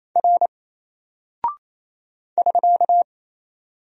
Morse code, heard as a keyed tone, about 27 wpm, 700 Hz. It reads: R <SK>
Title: Single Character (Letter:Number:Punctuation:Prosign) 28wpm